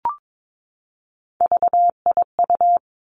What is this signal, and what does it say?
Morse code, heard as a keyed tone, about 22 wpm, 700 Hz. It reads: VIU
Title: Three Challenging Letter-Number Combinations 22wpm